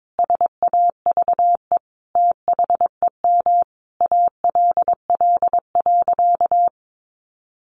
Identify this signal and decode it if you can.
Morse code, heard as a keyed tone, about 22 wpm, 700 Hz. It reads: SAVE THEM ALL.
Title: Sentences from Top 1100 Words 22wpm